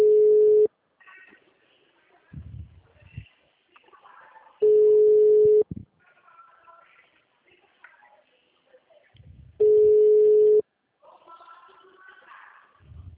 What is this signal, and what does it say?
Ringback tone (Europe, 425 Hz), with speech.